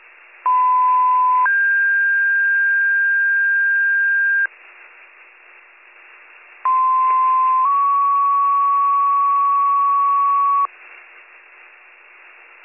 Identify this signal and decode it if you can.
Two-tone page, heard as a one-second tone, then a three-second tone, about 1010 then 1600 Hz, about 1040 then 1120 Hz. It is heard from 0.5 to 10.7 s.